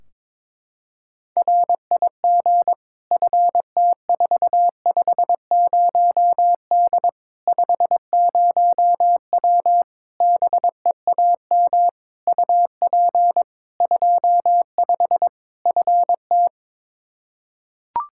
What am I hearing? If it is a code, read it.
Morse code, heard as a keyed tone, about 21 wpm, 700 Hz. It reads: RIG FT450D 50W BEAM UP 25 FT
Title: Practice QSO - Exchange 2 - Info - 22wpm